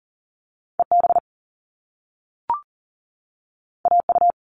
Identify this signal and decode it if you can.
Morse code, heard as a keyed tone, about 40 wpm, 700 Hz. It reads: EB AU
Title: All Cut Numbers - 2-Digit 40wpm